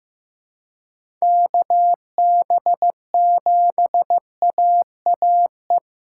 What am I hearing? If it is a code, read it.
Morse code, heard as a keyed tone, about 15 wpm, 700 Hz. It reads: KB7AAE
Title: USA Callsign DE Callsign - 6 Characters - 15wpm